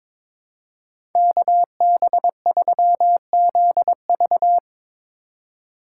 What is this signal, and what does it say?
Morse code, heard as a keyed tone, about 24 wpm, 700 Hz. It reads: KB3ZV
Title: Speed Racing - USA 2x2 Call Signs 22wpm